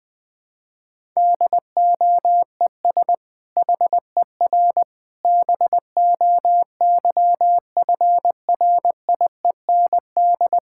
Morse code, heard as a keyed tone, about 20 wpm, 700 Hz. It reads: DOES HER BOYFRIEND
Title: Sentences from Top 1000 Words - Review Part 5 - 20wpm